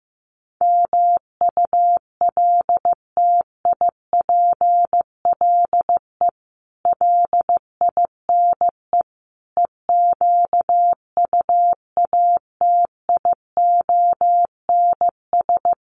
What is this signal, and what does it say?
Morse code, heard as a keyed tone, about 14 wpm, 700 Hz. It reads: MULTIPLE LINE EQUATIONS
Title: Sets of 5 Words 15wpm